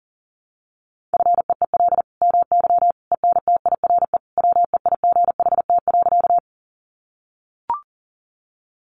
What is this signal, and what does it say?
Morse code, heard as a keyed tone, about 40 wpm, 700 Hz. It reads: FEEL MY ENTIRE WEIGHT.
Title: Sentences from Top 800 Words 40wpm